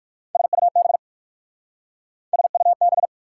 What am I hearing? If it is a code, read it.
Morse code, heard as a keyed tone, about 62 wpm, 690 Hz. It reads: SUB SUB